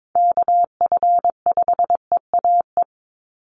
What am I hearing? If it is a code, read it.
Morse code, heard as a keyed tone, about 22 wpm, 700 Hz. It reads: KF5EAE
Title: USA Call Signs 22wpm